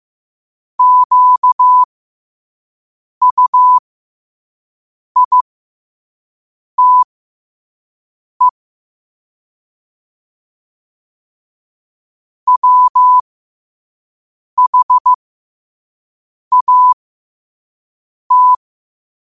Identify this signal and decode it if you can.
Morse code, heard as a keyed tone, about 15 wpm, spaced as for 5 wpm, 1000 Hz. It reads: QUITE WHAT
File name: qotdfiles / 5wpm